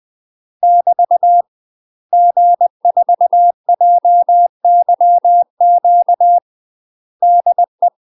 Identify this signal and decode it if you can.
Morse code, heard as a keyed tone, about 20 wpm, 700 Hz. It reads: <BT> G4JYQ DE
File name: UK MASTER QSOs TRAINING / 20WPM / UK MASTER QSOs 0035